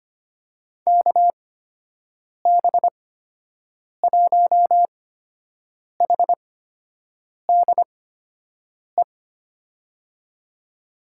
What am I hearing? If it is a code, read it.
Morse code, heard as a keyed tone, about 25 wpm, spaced as for 7 wpm, 700 Hz. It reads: KB1HDE
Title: USA Call Signs - 6 Characters - 8x Character Spacing - 25wpm